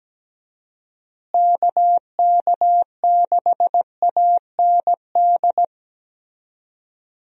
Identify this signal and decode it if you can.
Morse code, heard as a keyed tone, about 17 wpm, 700 Hz. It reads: KK6AND
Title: USA Call Signs - 6 Characters - Repeated 2x - 17wpm